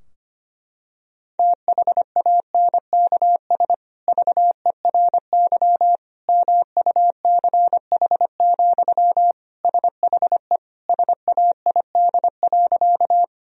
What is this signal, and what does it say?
Morse code, heard as a keyed tone, about 25 wpm, 700 Hz. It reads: THANKS VERY MUCH, SHE SAID.
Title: Sentences from Top 1000 Words - Review Part 3 - 25wpm